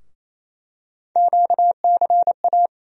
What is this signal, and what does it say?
Morse code, heard as a keyed tone, about 28 wpm, 700 Hz. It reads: QCA
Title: USA Call Sign Postfixes - 3 Letters 28wpm